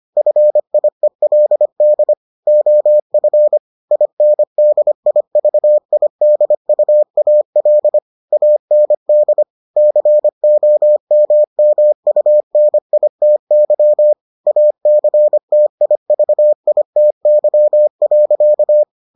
Morse code, heard as a keyed tone, about 25 wpm, 600 Hz. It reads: FIELD OF INDIVIDUAL AND COMMUNITY ACTIVITY.